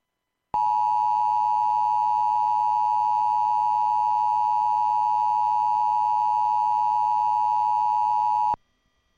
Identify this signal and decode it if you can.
Emergency Alert System (SAME) tones (the 853 and 960 Hz attention tone). It is heard from 0.5 to 8.6 s.